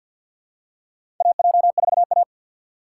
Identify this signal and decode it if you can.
Morse code, heard as a keyed tone, about 50 wpm, 700 Hz. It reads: AJ4A
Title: USA Call Signs - 4x Character Spacing - 50wpm